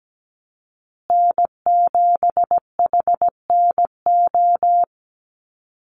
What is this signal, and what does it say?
Morse code, heard as a keyed tone, about 17 wpm, 700 Hz. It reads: N7HNO
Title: USA Call Signs Repeated 2x 17wpm